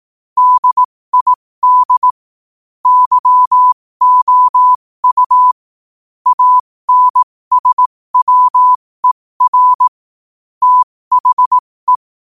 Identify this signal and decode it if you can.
Morse code, heard as a keyed tone, about 18 wpm, spaced as for 15 wpm, 1000 Hz. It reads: DID YOU ANSWER THE
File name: qotdfiles / 15wpm